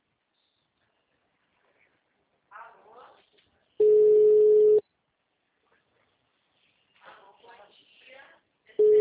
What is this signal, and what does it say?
Ringback tone (Europe, 425 Hz), with speech.